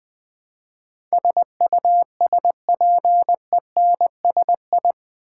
Morse code, heard as a keyed tone, about 20 wpm, 700 Hz. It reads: SUSPENSI
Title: Build A Word - 4 Letters At A Time - 10-Letter Words - 20wpm